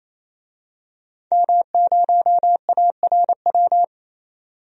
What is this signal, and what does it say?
Morse code, heard as a keyed tone, about 28 wpm, 700 Hz. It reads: M0ARW
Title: DXCC-100 Call Signs - 5 Characters - 28wpm